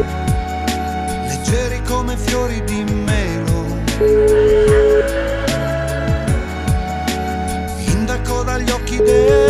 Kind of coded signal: telephone call progress tone

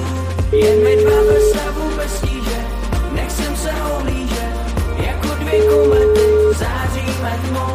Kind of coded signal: telephone call progress tone

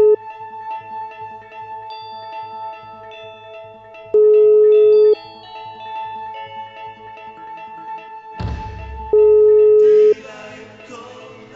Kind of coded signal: telephone call progress tone

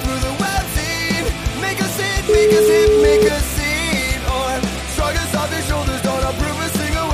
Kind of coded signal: telephone call progress tone

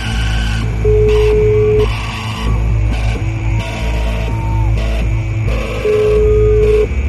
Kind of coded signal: telephone call progress tone

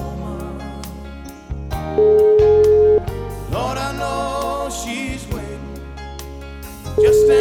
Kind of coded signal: telephone call progress tone

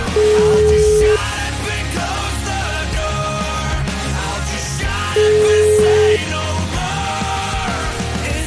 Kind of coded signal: telephone call progress tone

